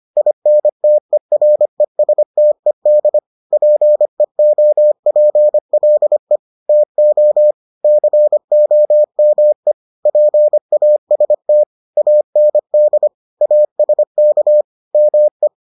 Morse code, heard as a keyed tone, about 25 wpm, 600 Hz. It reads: INTERESTED PEOPLE TO COME PAST AND ASK ME